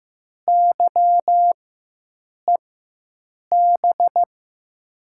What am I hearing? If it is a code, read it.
Morse code, heard as a keyed tone, about 15 wpm, spaced as for 7 wpm, 700 Hz. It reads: YEB
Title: USA Call Signs - 6 Characters - 4x Character Spacing - 15wpm